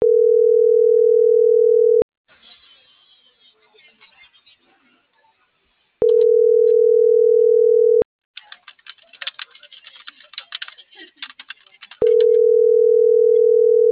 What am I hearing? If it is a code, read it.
Ringback tone (North America), 440 + 480 Hz.
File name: CENTRAL NMP / out patricia.martineznmp - .42674